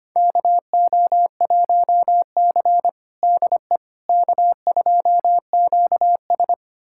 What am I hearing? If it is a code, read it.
Morse code, heard as a keyed tone, about 25 wpm, 700 Hz. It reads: KO1C DE K2QS
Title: USA Callsign DE Callsign - 4 Characters - 25wpm